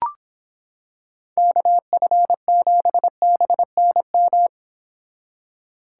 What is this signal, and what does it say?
Morse code, heard as a keyed tone, about 26 wpm, 700 Hz. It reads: KF7BNM